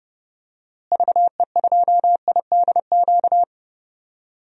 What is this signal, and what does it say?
Morse code, heard as a keyed tone, about 30 wpm, 700 Hz. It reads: VE2IDQ
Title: DXCC-100 Call Signs - 6 Characters - Repeated 2x - 30wpm